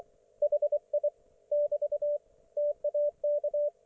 Morse code, heard as a keyed tone, about 24 wpm, 590 Hz, with radio noise. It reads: HI <BT> TAK
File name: Archiv / CUC212 / ok1fhi cul212